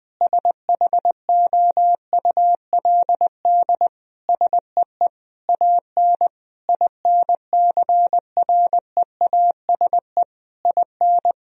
Morse code, heard as a keyed tone, about 20 wpm, 700 Hz. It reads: SHOULD SEE AN INCREASE IN
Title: Sentences from Top 1000 Words - Review Part 4 - 20wpm